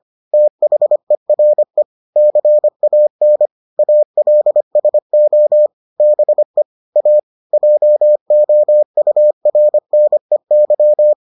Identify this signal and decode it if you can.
Morse code, heard as a keyed tone, about 25 wpm, 600 Hz. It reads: THERE CAN ALSO BE A JOURNEY